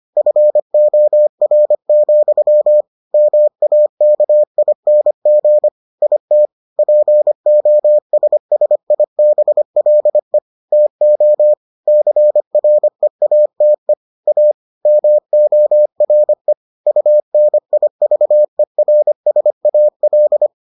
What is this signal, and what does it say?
Morse code, heard as a keyed tone, about 25 wpm, 600 Hz. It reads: FOR, MAKING IT POSSIBLE TO CREATE A MORE UNIVERSAL